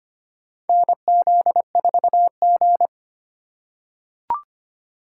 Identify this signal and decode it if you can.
Morse code, heard as a keyed tone, about 25 wpm, 700 Hz. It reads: NZ4G
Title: USA Call Signs - 4 Characters - 3x Character Spacing - 25wpm